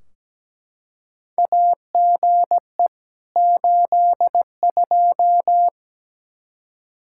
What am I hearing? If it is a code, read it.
Morse code, heard as a keyed tone, about 17 wpm, 700 Hz. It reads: AGE 82
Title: Practice QSO - Exchange 3 - More Info - 17wpm